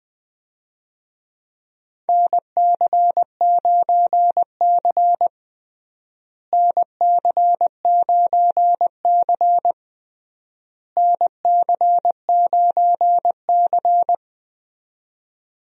Morse code, heard as a keyed tone, about 20 wpm, 700 Hz. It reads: NC9C NC9C NC9C
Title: USA Call Signs - 4 Characters - Repeated 3x - 20wpm